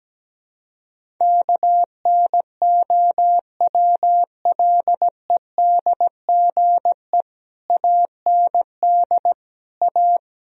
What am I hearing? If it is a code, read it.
Morse code, heard as a keyed tone, about 17 wpm, 700 Hz. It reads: KNOWLEDGE AND A
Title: Sentences from Top 1000 Words - Review Part 4 - 17wpm